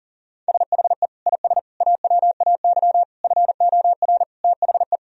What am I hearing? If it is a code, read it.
Morse code, heard as a keyed tone, about 40 wpm, 700 Hz. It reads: SHE IS AWAY FOR THE
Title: Sentences from Top 1000 Words - Review Part 2 - 40wpm